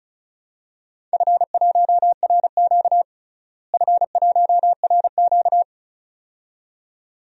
Morse code, heard as a keyed tone, about 34 wpm, 700 Hz. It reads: F1RQ F1RQ
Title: DXCC-100 Call Signs - 4 Characters - Repeated 2x - 35wpm